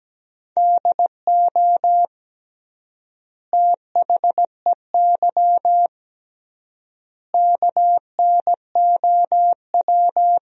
Morse code, heard as a keyed tone, about 17 wpm, 700 Hz. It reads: DO THEY KNOW
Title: 3x Word-Spacing - Sentences from Top 100 Words - 17wpm